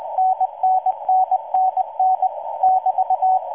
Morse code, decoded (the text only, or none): NNNNN <BT>